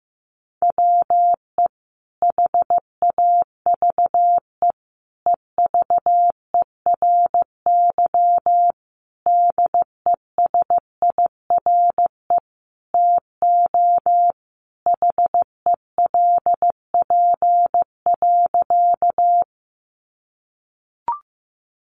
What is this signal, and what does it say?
Morse code, heard as a keyed tone, about 15 wpm, 700 Hz. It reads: WE HAVE EVERY DESIRE TO HELP.
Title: Sentences from Top 1100 Words 15wpm